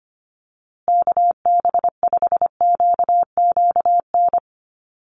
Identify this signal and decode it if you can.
Morse code, heard as a keyed tone, about 25 wpm, 700 Hz. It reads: KB5QQN
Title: USA Call Signs Repeated 3x 25wpm